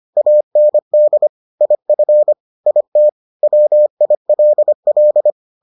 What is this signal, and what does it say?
Morse code, heard as a keyed tone, about 25 wpm, 600 Hz. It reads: AND IF IT WILL